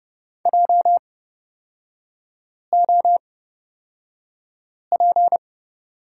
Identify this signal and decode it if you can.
Morse code, heard as a keyed tone, about 31 wpm, spaced as for 5 wpm, 690 Hz. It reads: JOP
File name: Lesson 37e - Letter x - Three Characters 30wpm